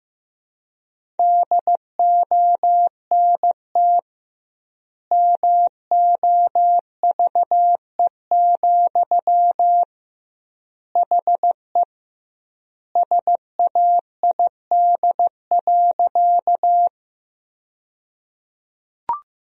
Morse code, heard as a keyed tone, about 16 wpm, 700 Hz. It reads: DONT MOVE, HE SAID.
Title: 2x Word-Spacing - Sentences from Top 300 Words - 15wpm